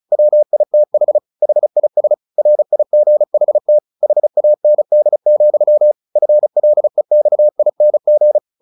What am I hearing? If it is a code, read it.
Morse code, heard as a keyed tone, about 34 wpm, 600 Hz. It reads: WITH HIS RIGHT HAND, FLEXING
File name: the old man and the sea 2 35wpm